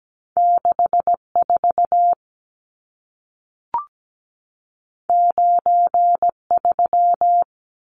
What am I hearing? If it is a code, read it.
Morse code, heard as a keyed tone, about 17 wpm, 700 Hz. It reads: 64 93
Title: Numbers - 2-Digit 17wpm